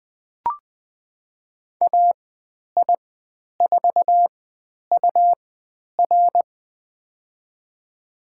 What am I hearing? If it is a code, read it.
Morse code, heard as a keyed tone, about 20 wpm, spaced as for 10 wpm, 700 Hz. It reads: AI4UR
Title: 109 Lesson 18g - Letter m - Call Signs 20wpm